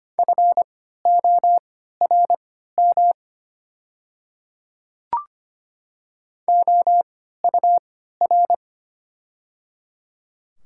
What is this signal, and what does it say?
Morse code, heard as a keyed tone, about 25 wpm, spaced as for 14 wpm, 690 Hz. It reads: FORM OUR
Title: 3x Character Spacing - Top 200 Words - 25wpm